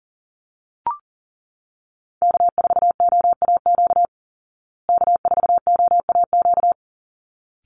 Morse code, heard as a keyed tone, about 39 wpm, 700 Hz. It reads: K4OAQ K4OAQ
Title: Call Signs Repeated 3x - 40wpm